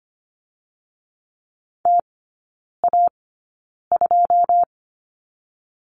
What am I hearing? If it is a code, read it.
Morse code, heard as a keyed tone, about 25 wpm, spaced as for 9 wpm, 690 Hz. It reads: TA2
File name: Lesson 14e - Number 2 - Three Characters 25wpm